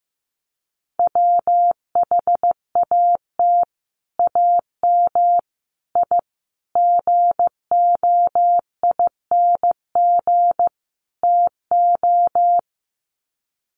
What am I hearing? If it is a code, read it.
Morse code, heard as a keyed tone, about 15 wpm, 700 Hz. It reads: WHAT AM I GOING TO
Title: Sets of 5 Words Encore 15wpm